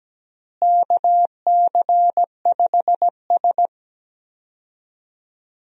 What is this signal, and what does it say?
Morse code, heard as a keyed tone, about 17 wpm, 700 Hz. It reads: KC5S
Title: USA Call Signs - 4 Characters - Repeated 2x - 17wpm